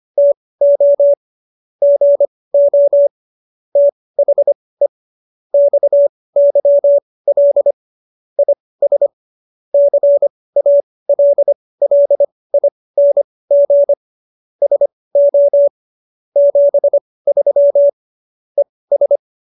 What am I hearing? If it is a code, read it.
Morse code, heard as a keyed tone, about 25 wpm, spaced as for 18 wpm, 570 Hz. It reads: TO GO THE XYL IS CALLING SO 73 ES